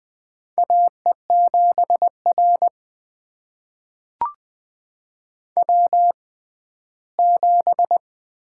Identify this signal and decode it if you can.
Morse code, heard as a keyed tone, about 20 wpm, spaced as for 14 wpm, 700 Hz. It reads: AE7R W7
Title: USA Call Signs - 6x Character Spacing - 20wpm